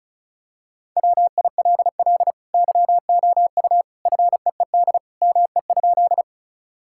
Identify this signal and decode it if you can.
Morse code, heard as a keyed tone, about 34 wpm, 700 Hz. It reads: WILL YOU FEED ME?
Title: Sentences from Top 1600 Words 35wpm